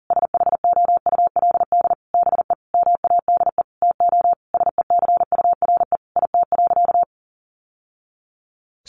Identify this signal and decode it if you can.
Morse code, heard as a keyed tone, about 40 wpm, 700 Hz. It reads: SHOULD BE MADE TO SECURE IT.
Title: Sentences from Top 1200 Words 40wpm